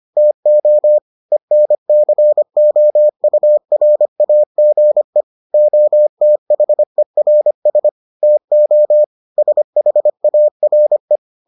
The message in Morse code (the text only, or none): TO ENCOURAGE OTHERS TO SHARE